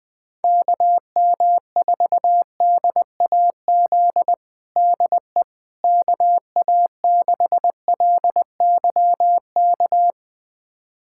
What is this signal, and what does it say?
Morse code, heard as a keyed tone, about 20 wpm, 700 Hz. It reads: KM4DAZ DE KA6LYK
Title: USA Callsign DE Callsign 20wpm